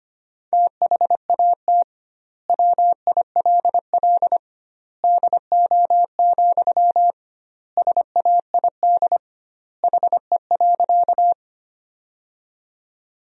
Morse code, heard as a keyed tone, about 25 wpm, 700 Hz. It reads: THAT WILL DO, SAID HE.
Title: 2x Word-Spacing - Sentences from Top 100 Words - 25wpm